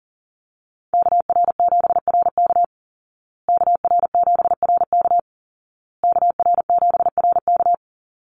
Morse code, heard as a keyed tone, about 40 wpm, 700 Hz. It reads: KR7RK KR7RK KR7RK
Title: Call Signs Repeated 2x - 40wpm